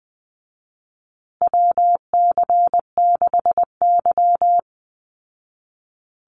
Morse code, heard as a keyed tone, about 20 wpm, 700 Hz. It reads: WC6Y
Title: Call Signs Repeated 2x - 20wpm